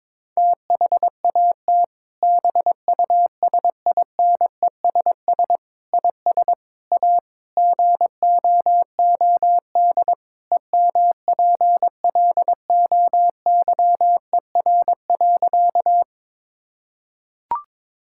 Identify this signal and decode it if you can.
Morse code, heard as a keyed tone, about 22 wpm, 700 Hz. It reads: THAT BUSINESS IS A GOOD EMPLOYER.
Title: Sentences from Top 1600 Words 22wpm